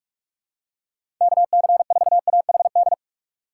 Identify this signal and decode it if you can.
Morse code, heard as a keyed tone, about 46 wpm, 690 Hz. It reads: KC4AHD
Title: USA Call Signs - 6 Characters - 3x Character Spacing - 45wpm